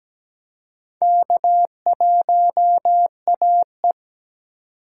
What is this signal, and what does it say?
Morse code, heard as a keyed tone, about 17 wpm, 700 Hz. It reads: K1AE
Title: K1USN SST QSO Practice - 17wpm